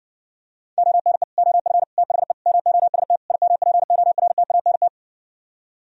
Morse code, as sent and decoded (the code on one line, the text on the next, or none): --- -. / --- ..-. / - .... / -- --- ... - / .. -- .--. --- .-. - .- -. -
ONE OF THE MOST IMPORTANT